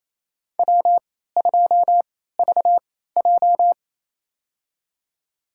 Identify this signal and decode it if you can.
Morse code, heard as a keyed tone, about 28 wpm, spaced as for 16 wpm, 700 Hz. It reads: W2VJ
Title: USA Call Signs - 4 Characters - 3x Character Spacing - 28wpm